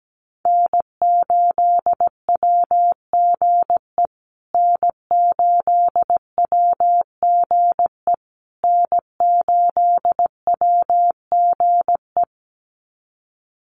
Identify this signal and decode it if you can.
Morse code, heard as a keyed tone, about 17 wpm, 700 Hz. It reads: N8WGE N8WGE N8WGE
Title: USA Call Signs Repeated 3x 17wpm